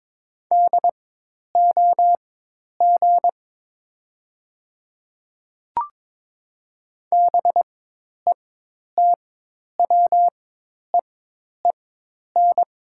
Morse code, heard as a keyed tone, about 22 wpm, spaced as for 10 wpm, 700 Hz. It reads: DOG BETWEEN
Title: 4x Character Spacing - Top 500 Words - 22wpm